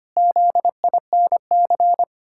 Morse code, heard as a keyed tone, about 25 wpm, 700 Hz. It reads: ZINC
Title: Pangram Challenge v2 - 25wpm